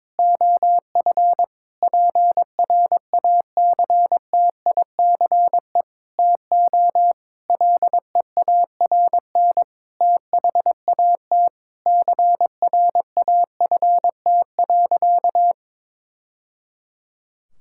Morse code, heard as a keyed tone, about 22 wpm, 700 Hz. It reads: OF PRACTICE TO LEARN THAT CRAFT.
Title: Sentences from Top 1000 Words - Review Part 6 - 22wpm